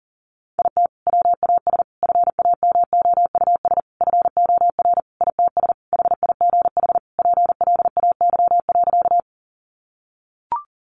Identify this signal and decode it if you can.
Morse code, heard as a keyed tone, about 40 wpm, 700 Hz. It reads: IT WAS FAMOUS FOR ITS HIGH PLAY.
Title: Sentences from Top 1500 Words 40wpm